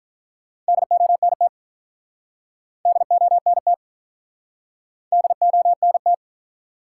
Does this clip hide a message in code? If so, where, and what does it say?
Yes: Morse code from 0.7 to 6.1 s, heard as a keyed tone, about 48 wpm, 700 Hz. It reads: DONT DONT DONT